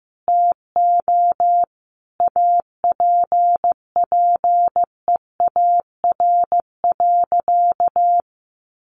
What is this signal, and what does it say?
Morse code, heard as a keyed tone, about 15 wpm, 700 Hz. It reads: TO APPEAR.
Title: Sentences from Top 1200 Words 15wpm